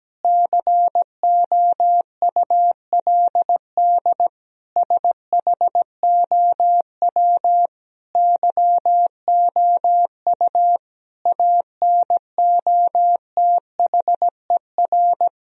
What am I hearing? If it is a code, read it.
Morse code, heard as a keyed tone, about 17 wpm, 700 Hz. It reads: COULD SHOW YOU ANOTHER
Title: Sentences from Top 1000 Words - Review Part 3 - 17wpm